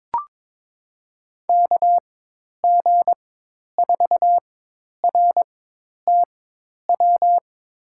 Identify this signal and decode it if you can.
Morse code, heard as a keyed tone, about 22 wpm, spaced as for 10 wpm, 700 Hz. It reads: KG4RTW
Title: USA Call Signs - 6 Characters - 4x Character Spacing - 22wpm